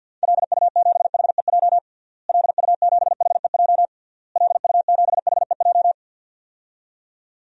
Morse code, heard as a keyed tone, about 52 wpm, 700 Hz. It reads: LU7HEJ LU7HEJ LU7HEJ
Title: DXCC-100 Call Signs - 6 Characters - Repeated 3x - 50wpm